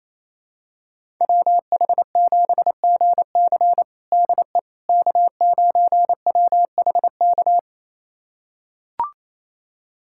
Morse code, heard as a keyed tone, about 28 wpm, 700 Hz. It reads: WH7GC DE K9WHK
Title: USA Callsign DE Callsign - 5 Characters - 28wpm